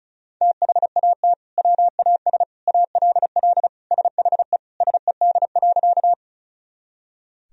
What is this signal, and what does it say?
Morse code, heard as a keyed tone, about 36 wpm, 700 Hz. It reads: THAT WAS ALL SHE SED.
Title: Sentences from Top 100 Words and Abbreviations 35wpm